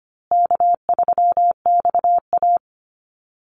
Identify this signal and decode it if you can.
Morse code, heard as a keyed tone, about 25 wpm, 700 Hz. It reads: K3XA
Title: Call Signs Repeated 2x - 25wpm